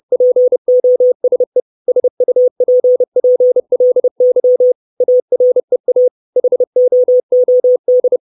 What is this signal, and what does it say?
Morse code, heard as a keyed tone, about 30 wpm, 500 Hz. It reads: POSE SUPPLY AREA HOOD
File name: 632 Words 30WPM